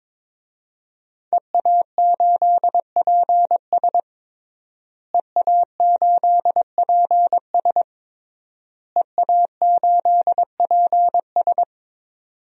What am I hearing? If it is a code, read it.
Morse code, heard as a keyed tone, about 22 wpm, 700 Hz. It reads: EA8PS EA8PS EA8PS
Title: DXCC-100 Call Signs - 5 Characters - Repeated 3x - 22wpm